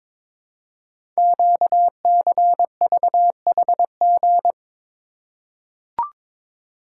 Morse code, heard as a keyed tone, about 22 wpm, 700 Hz. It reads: QCVHG
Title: Code Groups - 5 Letters - 22wpm